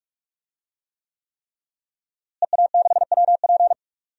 Morse code, heard as a keyed tone, about 41 wpm, 700 Hz. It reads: EA6WP
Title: DXCC-100 Call Signs - 5 Characters - Repeated 3x - 45wpm